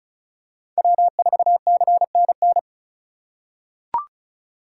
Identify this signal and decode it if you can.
Morse code, heard as a keyed tone, about 36 wpm, 690 Hz. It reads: W4CNN
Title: USA Call Signs - 5 Characters - 2x Character Spacing - 35wpm